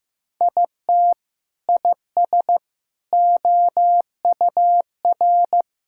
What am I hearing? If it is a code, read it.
Morse code, heard as a keyed tone, about 16 wpm, 700 Hz. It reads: IT IS OUR
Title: Sentences from Top 1000 Words - Review Part 6 - 15wpm